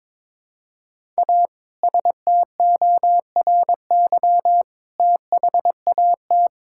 Morse code, heard as a keyed tone, about 22 wpm, 700 Hz. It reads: A STORY THAT
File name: practice / 22wpm / Sentences with 2-Word Phrases - 22wpm